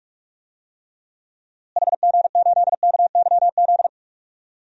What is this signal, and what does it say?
Morse code, heard as a keyed tone, about 42 wpm, 700 Hz. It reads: HG8KYZ
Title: DXCC-100 Call Signs - 6 Characters - Repeated 2x - 45wpm